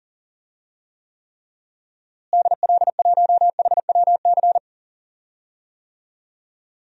Morse code, heard as a keyed tone, about 40 wpm, 700 Hz. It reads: DL1HWC
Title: DXCC-100 Call Signs - 6 Characters - Repeated 3x - 40wpm